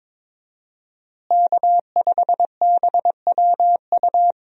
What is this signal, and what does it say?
Morse code, heard as a keyed tone, about 22 wpm, 700 Hz. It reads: K5BWU
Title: USA Call Signs - 5 Characters - Repeated 2x - 22wpm